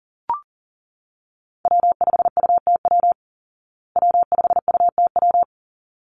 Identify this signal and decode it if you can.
Morse code, heard as a keyed tone, about 40 wpm, 700 Hz. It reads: W5UTW W5UTW
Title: Call Signs Repeated 3x - 40wpm